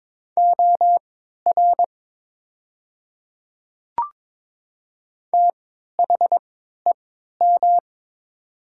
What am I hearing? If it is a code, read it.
Morse code, heard as a keyed tone, about 22 wpm, spaced as for 13 wpm, 700 Hz. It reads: OR THEM
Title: 3x Character Spacing - Top 200 Words - 22wpm